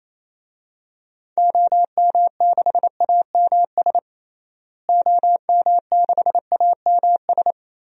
Morse code, heard as a keyed tone, about 28 wpm, 700 Hz. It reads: OM6AMS OM6AMS
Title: DXCC-100 Call Signs - 6 Characters - Repeated 2x - 28wpm